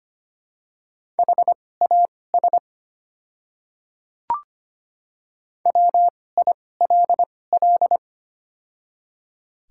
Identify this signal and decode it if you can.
Morse code, heard as a keyed tone, about 25 wpm, spaced as for 18 wpm, 690 Hz. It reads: HAS WILL
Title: 2x Character Spacing - Top 100 Words - 25wpm